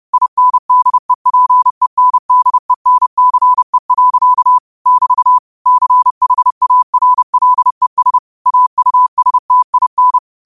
Morse code, heard as a keyed tone, about 30 wpm, 1000 Hz. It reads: INDEPENDENCE. <BT> CHARLES AUSTIN
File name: qotdfiles / 30wpm